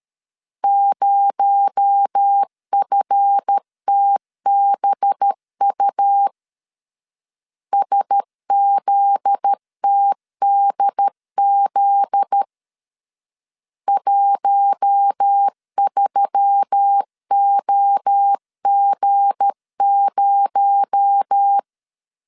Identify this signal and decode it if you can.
Morse code, heard as a keyed tone, about 13 wpm, 800 Hz. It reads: 0FTBU SZTDZ 13OG0
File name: LEZIONI / 7 GF9Z / 7 GF9Z 12wpm 2147